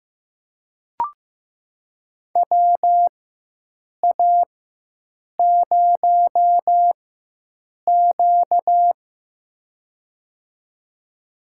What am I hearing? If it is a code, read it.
Morse code, heard as a keyed tone, about 15 wpm, spaced as for 7 wpm, 700 Hz. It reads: WA0Q
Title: USA Call Signs - 4 Characters - 4x Character Spacing - 15wpm